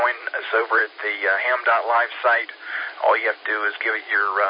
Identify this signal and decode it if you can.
Emergency Alert System (SAME) tones, heard as harsh buzzing bursts, with speech. It is heard from 0.7 to 4.3 s.